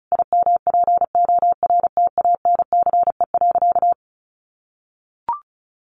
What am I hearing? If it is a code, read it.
Morse code, heard as a keyed tone, about 34 wpm, 700 Hz. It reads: IMPORTANCE.
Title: Sentences from Top 1200 Words 35wpm